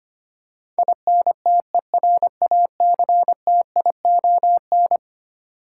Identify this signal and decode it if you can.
Morse code, heard as a keyed tone, about 25 wpm, 700 Hz. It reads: INTERACTION
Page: morse-code-ninja-practice-25wpm